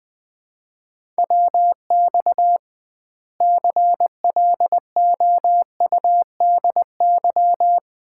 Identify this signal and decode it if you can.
Morse code, heard as a keyed tone, about 20 wpm, 700 Hz. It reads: WX CLOUDY
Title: Common QSO Phrases - 2x Word-Spacing - 20wpm